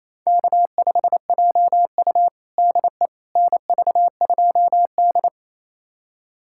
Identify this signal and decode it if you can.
Morse code, heard as a keyed tone, about 28 wpm, 700 Hz. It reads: K5JU DE NV2D
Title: USA Callsign DE Callsign - 4 Characters - 28wpm